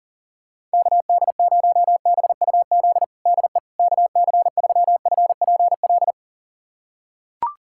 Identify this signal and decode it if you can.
Morse code, heard as a keyed tone, about 40 wpm, 690 Hz. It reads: KD0BUZ DE KC3FPL
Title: USA Callsign DE Callsign - 6 Characters - 40wpm